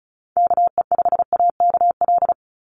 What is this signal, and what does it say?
Morse code, heard as a keyed tone, about 34 wpm, 700 Hz. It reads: KE5AKL
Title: NA SOTA Activator Callsigns 35wpm